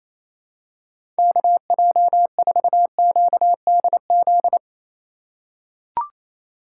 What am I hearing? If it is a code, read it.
Morse code, heard as a keyed tone, about 28 wpm, 700 Hz. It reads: KJ4QDZ
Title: USA Call Signs - 4x Character Spacing - 28wpm